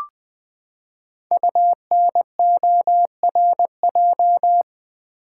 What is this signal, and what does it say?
Morse code, heard as a keyed tone, about 20 wpm, 700 Hz. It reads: UNORJ
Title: Code Groups - 5 Letters - 20wpm